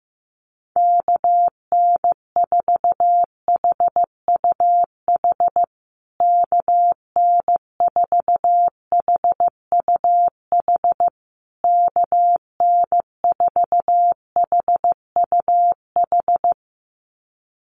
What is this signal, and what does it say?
Morse code, heard as a keyed tone, about 15 wpm, 700 Hz. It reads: KN4HUH KN4HUH KN4HUH
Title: USA Call Signs Repeated 3x 15wpm